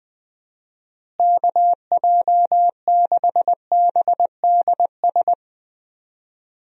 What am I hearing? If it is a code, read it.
Morse code, heard as a keyed tone, about 21 wpm, 700 Hz. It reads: KJ6BDS
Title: USA Call Signs - 6 Characters - Repeated 2x - 20wpm